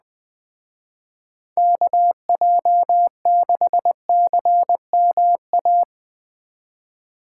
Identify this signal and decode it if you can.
Morse code, heard as a keyed tone, about 20 wpm, 700 Hz. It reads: KJ6CMA